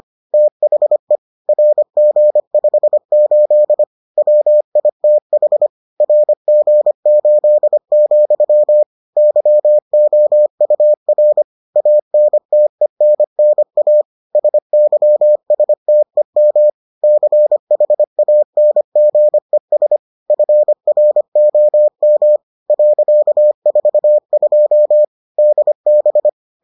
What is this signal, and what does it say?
Morse code, heard as a keyed tone, about 25 wpm, 600 Hz. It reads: THE RG58 WITH RG8, YOUR ANTENNA SYSTEM CHANGES FROM .42 DB